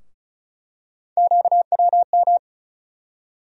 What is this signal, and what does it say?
Morse code, heard as a keyed tone, about 34 wpm, 700 Hz. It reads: QWM
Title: USA Call Sign Postfixes - 3 Letters 35wpm